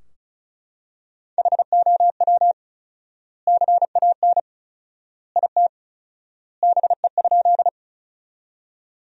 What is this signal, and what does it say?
Morse code, heard as a keyed tone, about 36 wpm, 700 Hz. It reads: HOW CAN IT BE?
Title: 4x Word-Spacing - Sentences from Top 100 Words - 35wpm